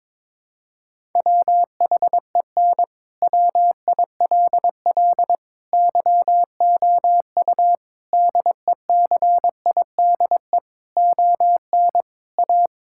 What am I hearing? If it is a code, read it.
Morse code, heard as a keyed tone, about 22 wpm, 700 Hz. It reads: WHEN WILL YOU DECIDE ON A
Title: Sentences from Top 1000 Words - Review Part 6 - 22wpm